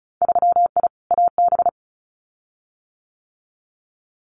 Morse code, heard as a keyed tone, about 36 wpm, 700 Hz. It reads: 3I AB
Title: Winter Field Day Exchanges Encore 35wpm